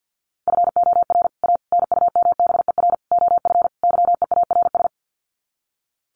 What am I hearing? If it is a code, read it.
Morse code, heard as a keyed tone, about 50 wpm, 700 Hz. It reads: FOR A NUMBER OF YEARS
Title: Sets of 5 Words Encore 50wpm